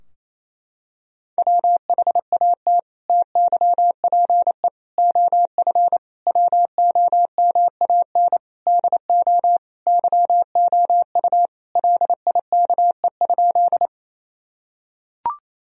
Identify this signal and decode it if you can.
Morse code, heard as a keyed tone, about 28 wpm, 700 Hz. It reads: WHAT TYPE OF WOMAN DO YOU LIKE?
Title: Sentences from Top 1000 Words - Review Part 3 - 28wpm